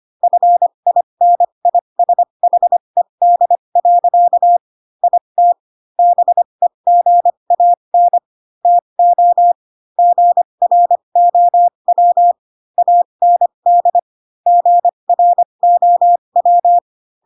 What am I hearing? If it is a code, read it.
Morse code, heard as a keyed tone, about 25 wpm, spaced as for 22 wpm, 700 Hz. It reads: FINISHED. IT BEGAN TO GROW AND GROW
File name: Pinocchio / Pinocchio ch3 18 25 ik1hge